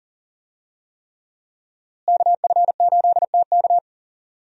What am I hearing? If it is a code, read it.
Morse code, heard as a keyed tone, about 40 wpm, 690 Hz. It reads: KF8TK
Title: USA Call Signs - 5 Characters - Repeated 3x - 40wpm